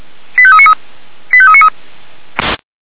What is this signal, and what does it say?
Five-tone selective call (ZVEI), heard as quick runs of five tones, tones of about 70 ms. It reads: 85393 85393